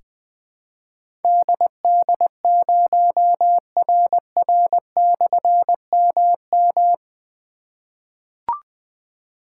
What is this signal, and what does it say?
Morse code, heard as a keyed tone, about 20 wpm, 700 Hz. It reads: DD0RR/MM
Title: Difficult Call Signs 20wpm